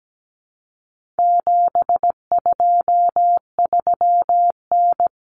Morse code, heard as a keyed tone, about 17 wpm, 700 Hz. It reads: 723N
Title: Common Cut Numbers - 4-Digit 17wpm